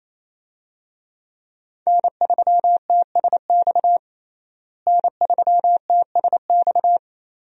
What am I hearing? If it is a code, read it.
Morse code, heard as a keyed tone, about 27 wpm, 700 Hz. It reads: N3TSX N3TSX
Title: USA Call Signs - 5 Characters - Repeated 2x - 28wpm